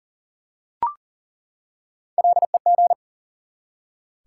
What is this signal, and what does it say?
Morse code, heard as a keyed tone, about 40 wpm, 700 Hz. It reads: LEG
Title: Three Letter Words - 40wpm